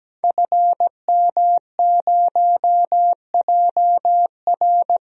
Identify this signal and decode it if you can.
Morse code, heard as a keyed tone, about 17 wpm, 700 Hz. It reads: FM0JR
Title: DXCC-100 Call Signs - 5 Characters - 17wpm